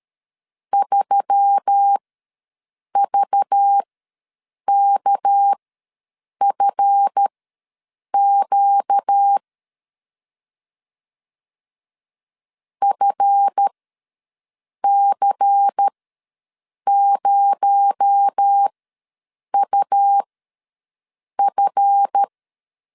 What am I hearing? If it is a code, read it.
Morse code, heard as a keyed tone, about 13 wpm, spaced as for 7 wpm, 800 Hz. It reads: 3VKFQ FC0UF
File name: LEZIONI / 7 GF9Z / 7 GF ws 12wpm 3140